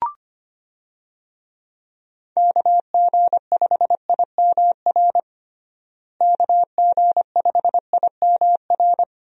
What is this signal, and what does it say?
Morse code, heard as a keyed tone, about 25 wpm, 700 Hz. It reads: KG5IMR KG5IMR
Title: USA Call Signs - 6 Characters - Repeated 2x - 25wpm